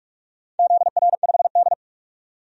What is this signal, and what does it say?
Morse code, heard as a keyed tone, about 45 wpm, 700 Hz. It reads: ZR5D
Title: DXCC-100 Call Signs - 4 Characters - 45wpm